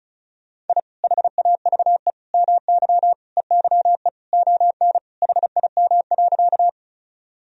Morse code, heard as a keyed tone, about 36 wpm, 700 Hz. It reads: I HAVE MY EYE ON HIM.
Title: Sentences from Top 1000 Words - Review Part 1 - 35wpm